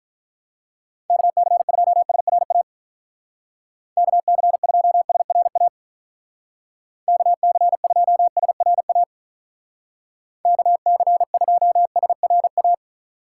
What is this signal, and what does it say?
Morse code, heard as a keyed tone, about 44 wpm, 700 Hz. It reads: KC2SRA KC2SRA KC2SRA KC2SRA
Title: Speed Racing - USA 2x3 Call Signs 35wpm